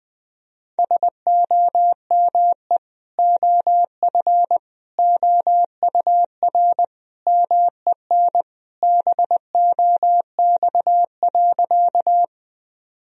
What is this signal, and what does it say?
Morse code, heard as a keyed tone, about 21 wpm, 700 Hz. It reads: SOME OF OUR MEN BOX.
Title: Sentences from Top 1000 Words - Review Part 2 - 20wpm